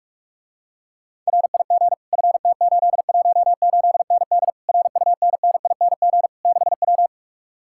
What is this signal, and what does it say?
Morse code, heard as a keyed tone, about 44 wpm, 700 Hz. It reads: RIG FT818ND RUNNING 6W